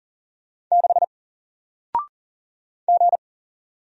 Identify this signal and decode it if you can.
Morse code, heard as a keyed tone, about 40 wpm, 700 Hz. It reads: 6 G
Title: Single Letter-Number 40wpm